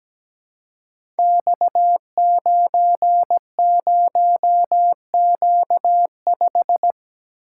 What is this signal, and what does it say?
Morse code, heard as a keyed tone, about 17 wpm, 700 Hz. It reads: X90Q5
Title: Code Groups - 5 Letter:Numbers - 17wpm